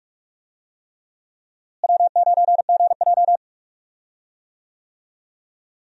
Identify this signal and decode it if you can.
Morse code, heard as a keyed tone, about 42 wpm, 700 Hz. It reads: W9GJ
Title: USA Call Signs - 4 Characters - Repeated 3x - 45wpm